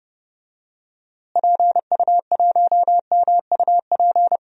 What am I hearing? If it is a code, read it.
Morse code, heard as a keyed tone, about 30 wpm, 700 Hz. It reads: PU1MUP
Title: DXCC-100 Call Signs - 6 Characters - Repeated 2x - 30wpm